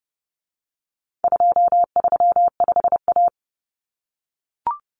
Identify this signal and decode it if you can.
Morse code, heard as a keyed tone, about 30 wpm, 700 Hz. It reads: 235A
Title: Common Cut Numbers - 4-Digit 30wpm